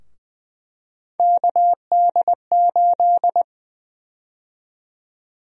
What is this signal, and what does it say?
Morse code, heard as a keyed tone, about 20 wpm, 700 Hz. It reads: KD8
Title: USA Call Sign Prefixes - Repeated 2x - 20wpm